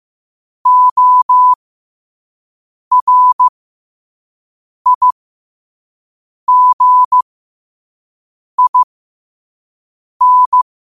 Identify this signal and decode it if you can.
Morse code, heard as a keyed tone, about 15 wpm, spaced as for 5 wpm, 1000 Hz. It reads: ORIGIN
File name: qotdfiles / 5wpm